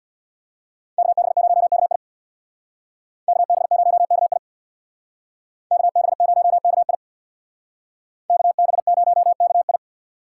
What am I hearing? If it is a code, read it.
Morse code, heard as a keyed tone, about 59 wpm, 700 Hz. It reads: KB0KI KB0KI KB0KI KB0KI
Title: Speed Racing - USA Call Signs 50wpm